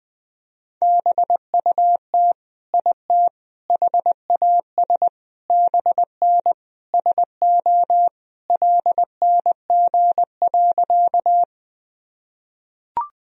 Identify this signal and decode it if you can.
Morse code, heard as a keyed tone, about 21 wpm, 700 Hz. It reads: BUT IT HAS BN SO LNG.
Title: Sentences from Top 100 Words and Abbreviations 20wpm